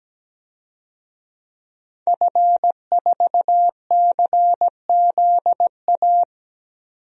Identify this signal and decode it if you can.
Morse code, heard as a keyed tone, about 17 wpm, 700 Hz. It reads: F4CZA
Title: DXCC-100 Call Signs - 5 Characters - 17wpm